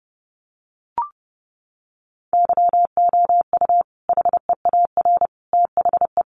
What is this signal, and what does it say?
Morse code, heard as a keyed tone, about 30 wpm, 700 Hz. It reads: YOU HEAR THE
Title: Sentences from Top 900 Words 30wpm